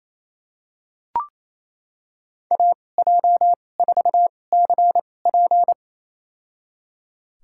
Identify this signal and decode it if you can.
Morse code, heard as a keyed tone, about 28 wpm, spaced as for 20 wpm, 700 Hz. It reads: AJ4CP
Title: USA Call Signs - 5 Characters - 2x Character Spacing - 28wpm